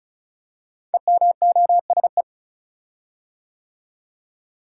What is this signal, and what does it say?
Morse code, heard as a keyed tone, about 36 wpm, 700 Hz. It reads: EMOSE
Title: Code Groups - 5 Letters - 35wpm